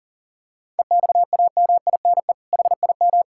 Morse code, heard as a keyed tone, about 40 wpm, 700 Hz. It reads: EXAMINE HIM
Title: Sentences from Top 1600 Words 40wpm